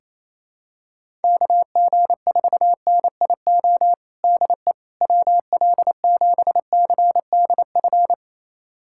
Morse code, heard as a keyed tone, about 28 wpm, 700 Hz. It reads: KG4NIO DE WL7CDF
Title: USA Callsign DE Callsign - 6 Characters - 28wpm